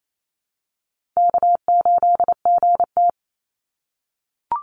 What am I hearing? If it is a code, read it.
Morse code, heard as a keyed tone, about 28 wpm, 700 Hz. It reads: K8GT
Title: Call Signs - 28wpm